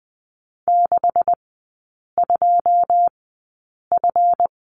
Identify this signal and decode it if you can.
Morse code, heard as a keyed tone, about 20 wpm, spaced as for 8 wpm, 700 Hz. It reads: 62F
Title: 138 Lesson 23e - Letter f - Three Characters 20wpm